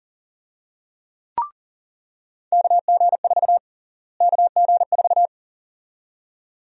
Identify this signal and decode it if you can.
Morse code, heard as a keyed tone, about 40 wpm, 700 Hz. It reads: KG4 KG4
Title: USA Call Sign Prefixes - Repeated 2x - 40wpm